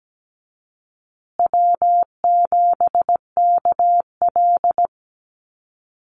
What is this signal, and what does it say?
Morse code, heard as a keyed tone, about 17 wpm, 700 Hz. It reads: W7KL
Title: USA Call Signs Repeated 2x 17wpm